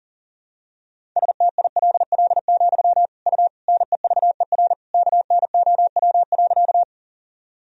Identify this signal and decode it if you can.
Morse code, heard as a keyed tone, about 40 wpm, 700 Hz. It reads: STILL, U NEVER KNOW.
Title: Sentences from Top 300 Words and Abbreviations 40wpm